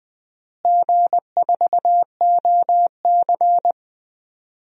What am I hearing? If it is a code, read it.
Morse code, heard as a keyed tone, about 21 wpm, 700 Hz. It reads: G4OC
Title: DXCC-100 Call Signs - 4 Characters - Repeated 3x - 20wpm